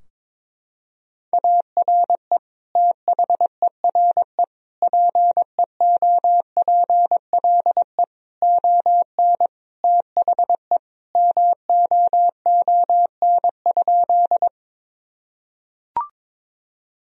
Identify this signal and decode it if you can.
Morse code, heard as a keyed tone, about 22 wpm, 700 Hz. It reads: ARE THERE PEOPLE ON THE MOON?
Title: Sentences from Top 1000 Words - Review Part 3 - 22wpm